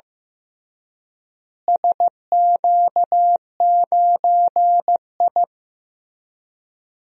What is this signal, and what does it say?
Morse code, heard as a keyed tone, about 15 wpm, 700 Hz. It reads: SQ9I
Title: DXCC-100 Call Signs - 4 Characters - Repeated 3x - 15wpm